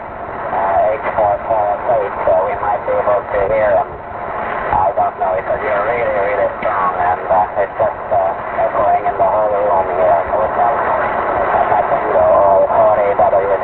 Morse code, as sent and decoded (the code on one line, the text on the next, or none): none